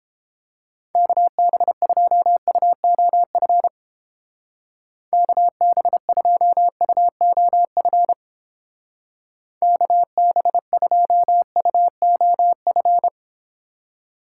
Morse code, heard as a keyed tone, about 29 wpm, 700 Hz. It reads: KB2UOF KB2UOF KB2UOF